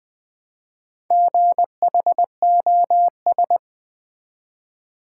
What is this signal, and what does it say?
Morse code, heard as a keyed tone, about 20 wpm, 700 Hz. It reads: GHOS
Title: Build A Word - 1 Letter At A Time - 5-Letter Words - 20wpm